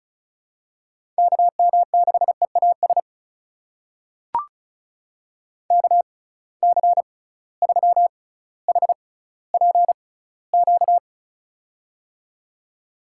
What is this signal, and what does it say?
Morse code, heard as a keyed tone, about 36 wpm, spaced as for 18 wpm, 700 Hz. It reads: KM6EAS KC3HPQ
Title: USA Call Signs - 6 Characters - 6x Character Spacing - 35wpm